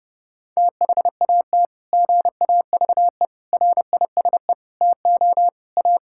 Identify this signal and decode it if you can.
Morse code, heard as a keyed tone, about 30 wpm, 700 Hz. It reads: THAT GAVE RISE TO A
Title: Sentences from Top 1000 Words - Review Part 6 - 30wpm